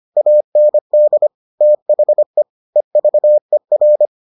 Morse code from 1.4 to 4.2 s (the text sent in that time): THE EVER